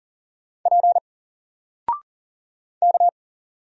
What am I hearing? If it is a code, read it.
Morse code, heard as a keyed tone, about 40 wpm, 700 Hz. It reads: P K
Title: Single Letter-Number Alt 40wpm